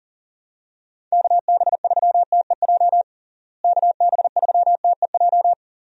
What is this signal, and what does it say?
Morse code, heard as a keyed tone, about 40 wpm, 700 Hz. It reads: KB3TEJ KB3TEJ
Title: USA Call Signs - 6 Characters - Repeated 2x - 40wpm